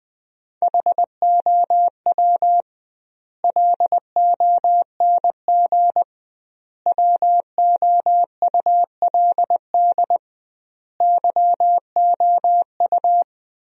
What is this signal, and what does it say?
Morse code, heard as a keyed tone, about 21 wpm, 700 Hz. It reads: HOW LONG WOULD YOU
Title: 2x Word-Spacing - Sentences from Top 500 Words - 20wpm